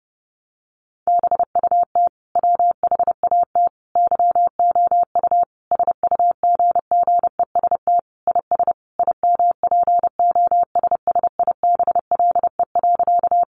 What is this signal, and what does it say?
Morse code, heard as a keyed tone, about 30 wpm, 700 Hz. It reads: BUT WHAT YOU SUGGEST IS IMPOSSIBLE.
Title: Sentences from Top 1300 Words 30wpm